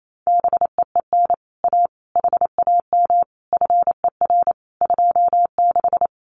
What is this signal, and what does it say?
Morse code, heard as a keyed tone, about 28 wpm, 700 Hz. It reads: BEEN A HAM FER 26
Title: Common QSO Phrases Encore 28wpm